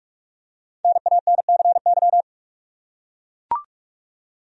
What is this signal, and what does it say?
Morse code, heard as a keyed tone, about 43 wpm, 700 Hz. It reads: NANCY
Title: USA Common Names - 45wpm